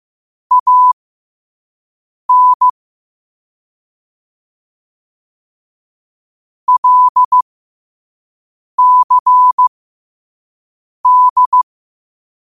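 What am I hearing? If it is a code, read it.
Morse code, heard as a keyed tone, about 15 wpm, spaced as for 5 wpm, 1000 Hz. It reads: AN LCD
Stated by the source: Morse code at 5 wpm.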